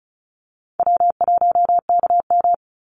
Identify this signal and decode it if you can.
Morse code, heard as a keyed tone, about 34 wpm, 700 Hz. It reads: W1KM
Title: Call Signs Repeated 2x - 35wpm